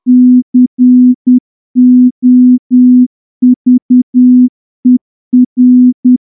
Morse code, heard as a keyed tone, about 10 wpm, 250 Hz. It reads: COVER
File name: morse-training-10WPM